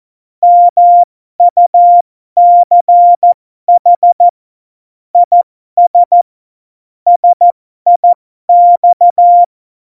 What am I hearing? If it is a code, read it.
Morse code, heard as a keyed tone, about 14 wpm, spaced as for 12 wpm, 700 Hz. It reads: MUCH IS SIX
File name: Morsekurs / Witze